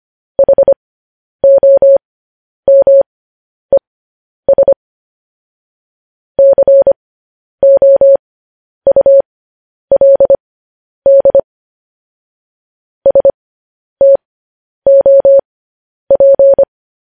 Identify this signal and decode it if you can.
Morse code, heard as a keyed tone, about 25 wpm, spaced as for 10 wpm, 550 Hz. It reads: HOMES COULD STOP
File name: podcast / News headlines-1x-25x10